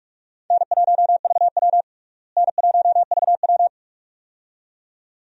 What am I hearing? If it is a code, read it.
Morse code, heard as a keyed tone, about 46 wpm, 700 Hz. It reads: N1VW N1VW
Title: USA Call Signs - 4 Characters - Repeated 2x - 45wpm